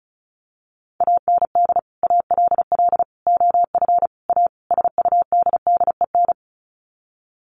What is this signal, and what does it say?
Morse code, heard as a keyed tone, about 34 wpm, 700 Hz. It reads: AND ALL OF A SUDDEN
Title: Sets of 5 Words 35wpm